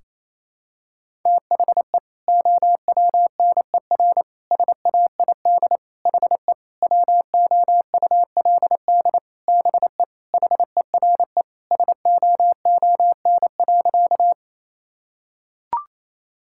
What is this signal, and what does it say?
Morse code, heard as a keyed tone, about 28 wpm, 700 Hz. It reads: THE OWNER SAID HE WOULD BE HERE SOON.
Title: Sentences from Top 1600 Words 28wpm